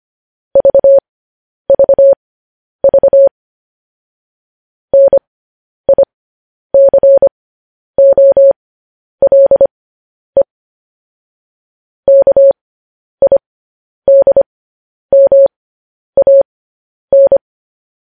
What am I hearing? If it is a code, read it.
Morse code, heard as a keyed tone, about 25 wpm, spaced as for 10 wpm, 550 Hz. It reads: VVV NICOLE KIDMAN
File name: podcast / News headlines-1x-25x10